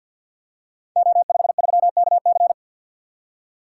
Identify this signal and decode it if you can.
Morse code, heard as a keyed tone, about 46 wpm, 700 Hz. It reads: O53KC
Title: Code Groups - 5 Letter:Numbers - 50wpm